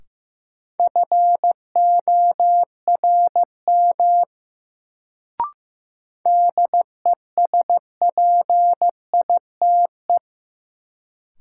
Morse code, heard as a keyed tone, about 15 wpm, 700 Hz. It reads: FORM DESPITE
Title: Sets of 4 Words - New Word Drill - 15wpm